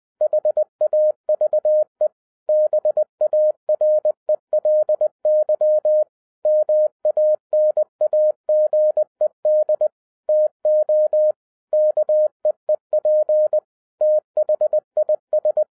Morse code, heard as a keyed tone, about 20 wpm, 610 Hz. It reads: HAVE BARELY MANAGED TO KEEP THIS
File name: 2012-01-mp3 / n16-THROUGH EIGHT MONTHS-20wpm